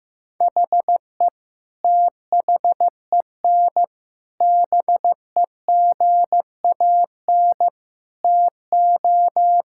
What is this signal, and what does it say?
Morse code, heard as a keyed tone, about 15 wpm, 700 Hz. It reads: HE THEN BEGAN TO
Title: Sentences from Top 1000 Words - Review Part 6 - 15wpm